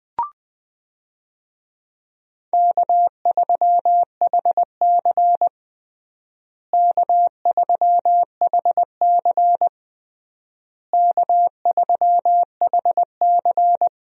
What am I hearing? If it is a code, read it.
Morse code, heard as a keyed tone, about 21 wpm, 700 Hz. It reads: K3HC K3HC K3HC
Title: USA Call Signs - 4 Characters - Repeated 3x - 20wpm